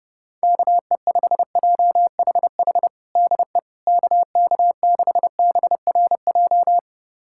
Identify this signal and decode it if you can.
Morse code, heard as a keyed tone, about 30 wpm, 700 Hz. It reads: KE5JHH DE KK6BRJ
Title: USA Callsign DE Callsign - 6 Characters - 30wpm